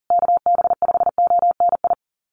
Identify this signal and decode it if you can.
Morse code, heard as a keyed tone, about 40 wpm, 700 Hz. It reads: KB5ONI
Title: USA Call Signs Repeated 3x 40wpm